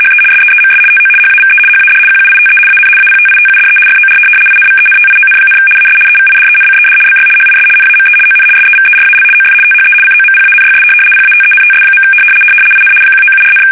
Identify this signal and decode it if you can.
RTTY, about 75 baud. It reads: HE SEEMS TO HAVE HAD QUITE AN APTITUDE FOR INVENTION, AS EVIDENCED BY HIS NAME ON SEVERAL OF THE COMPANYS PATENTS. IN 1974,
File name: fsk850-75B / teletype-corp